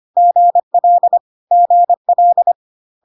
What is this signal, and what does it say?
Morse code, heard as a keyed tone, about 25 wpm, 700 Hz. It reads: GL GL
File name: cw / CW common abbreviations alphabetical repeated 3 times 25 25wpm ik1hge